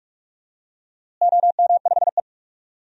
Morse code, heard as a keyed tone, about 46 wpm, 700 Hz. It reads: OM5E
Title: DXCC-100 Call Signs - 4 Characters - Repeated 3x - 45wpm